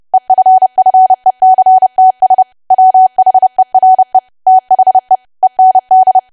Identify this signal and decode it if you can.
Morse code, heard as a keyed tone, about 30 wpm, 750 Hz. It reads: EFFECTS WHERE THE END
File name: cwa / W1AW-Level3 / 30WPM